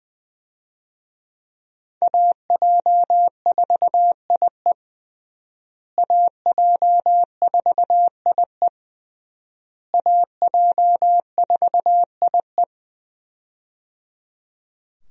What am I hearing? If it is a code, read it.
Morse code, heard as a keyed tone, about 21 wpm, 700 Hz. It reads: AJ4IE AJ4IE AJ4IE
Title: USA Call Signs - 5 Characters - Repeated 3x - 20wpm